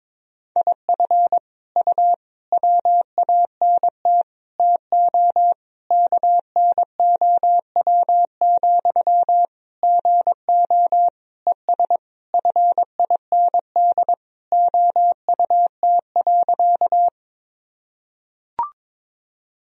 Morse code, heard as a keyed tone, about 22 wpm, 700 Hz. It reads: IF U WANT TO KNOW, GO ES FIND OUT.
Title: Sentences from Top 200 Words and Abbreviations 22wpm